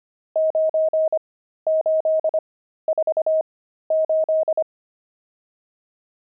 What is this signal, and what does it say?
Morse code, heard as a keyed tone, about 25 wpm, spaced as for 13 wpm, 630 Hz. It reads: 9848
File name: pangram-fs-13-cs-25-nums17